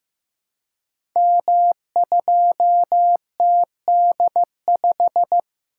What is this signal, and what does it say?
Morse code, heard as a keyed tone, about 15 wpm, 700 Hz. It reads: M2TD5
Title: Code Groups - 5 Letter:Numbers - 15wpm